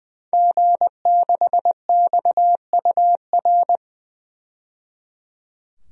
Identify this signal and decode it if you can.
Morse code, heard as a keyed tone, about 20 wpm, 700 Hz. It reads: G6XUR
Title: DXCC-100 Call Signs - 5 Characters - 20wpm